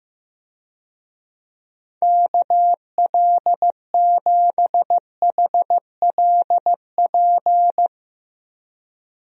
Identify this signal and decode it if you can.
Morse code, heard as a keyed tone, about 15 wpm, 700 Hz. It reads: KL7HLP
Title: DXCC-100 Call Signs - 6 Characters - Repeated 3x - 15wpm